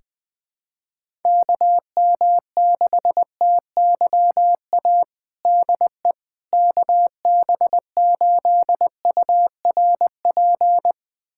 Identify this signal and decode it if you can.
Morse code, heard as a keyed tone, about 20 wpm, 700 Hz. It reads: KM6TYA DE KB8URP
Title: USA Callsign DE Callsign 20wpm